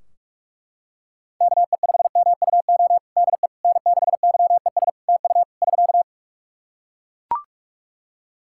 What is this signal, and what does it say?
Morse code, heard as a keyed tone, about 43 wpm, 700 Hz. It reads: KE5MUO DE N6YES TU <SK>
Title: K1USN SST QSO Practice 45wpm